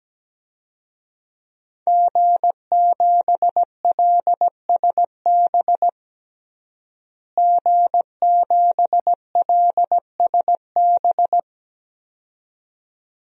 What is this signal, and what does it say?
Morse code, heard as a keyed tone, about 17 wpm, 700 Hz. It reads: G7LSB G7LSB
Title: DXCC-100 Call Signs - 5 Characters - Repeated 2x - 17wpm